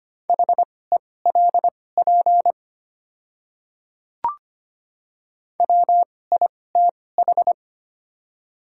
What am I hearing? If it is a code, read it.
Morse code, heard as a keyed tone, about 25 wpm, spaced as for 18 wpm, 700 Hz. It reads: HELP WITH
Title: 2x Character Spacing - Top 500 Words - 25wpm